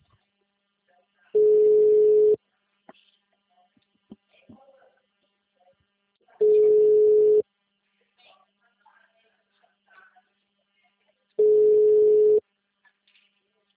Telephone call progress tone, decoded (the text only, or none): ringback tone